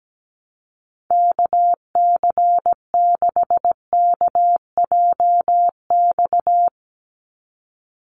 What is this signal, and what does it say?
Morse code, heard as a keyed tone, about 17 wpm, 700 Hz. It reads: KC6KJX
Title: USA Call Signs 17wpm